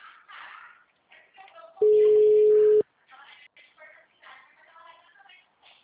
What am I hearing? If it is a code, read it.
Ringback tone (Europe, 425 Hz), with speech.